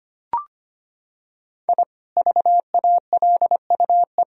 Morse code, heard as a keyed tone, about 25 wpm, 700 Hz. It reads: I VALUE
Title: Sentences from Top 1000 Words - Review Part 3 - 25wpm